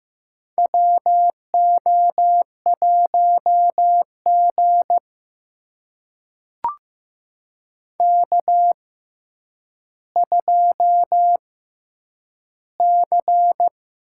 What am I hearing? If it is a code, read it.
Morse code, heard as a keyed tone, about 15 wpm, spaced as for 9 wpm, 700 Hz. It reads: WO1G K2C
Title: USA Call Signs - 6x Character Spacing - 15wpm